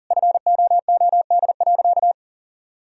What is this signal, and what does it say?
Morse code, heard as a keyed tone, about 40 wpm, 700 Hz. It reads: FOOD.
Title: Sentences from Top 1000 Words - Review Part 1 - 40wpm